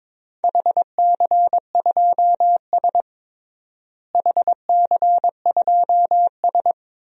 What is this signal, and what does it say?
Morse code, heard as a keyed tone, about 22 wpm, 700 Hz. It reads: HC2S HC2S
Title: DXCC-100 Call Signs - 4 Characters - Repeated 2x - 22wpm